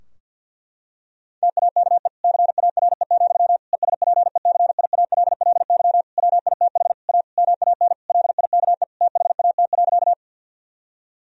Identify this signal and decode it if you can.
Morse code, heard as a keyed tone, about 50 wpm, 700 Hz. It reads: TAKE CARE, ESPECIALLY WITH A MAN LIKE THAT.
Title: Sentences from Top 1000 Words - Review Part 4 - 50wpm